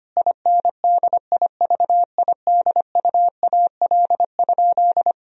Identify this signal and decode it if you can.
Morse code, heard as a keyed tone, about 25 wpm, 700 Hz. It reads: INDIVIDUAL?
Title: Sentences from Top 1000 Words - Review Part 4 - 25wpm